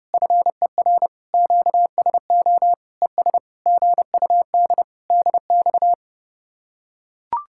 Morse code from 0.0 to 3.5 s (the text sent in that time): FER QSO ES